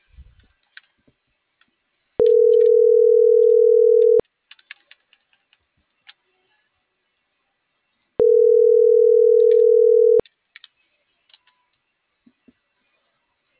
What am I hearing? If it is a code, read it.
Ringback tone (North America), 440 + 480 Hz.